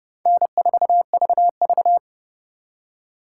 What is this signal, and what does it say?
Morse code, heard as a keyed tone, about 30 wpm, 700 Hz. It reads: N4VV
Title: USA Call Signs - 4 Characters - 3x Character Spacing - 30wpm